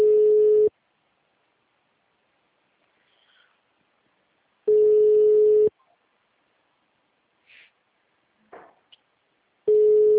Ringback tone (Europe, 425 Hz).
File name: out- -2011- - - .2527